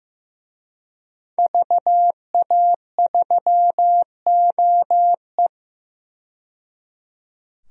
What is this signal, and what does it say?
Morse code, heard as a keyed tone, about 15 wpm, 700 Hz. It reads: VA3OE
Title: DXCC-100 Call Signs - 5 Characters - 15wpm